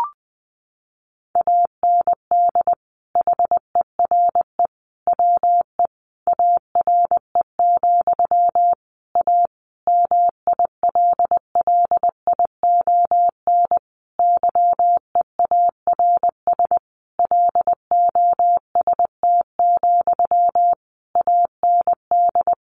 Morse code, heard as a keyed tone, about 20 wpm, 700 Hz. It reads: AND HERE WE ARE, A MILLION YEARS LOST, AND
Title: A Sound Of Thunder 20wpm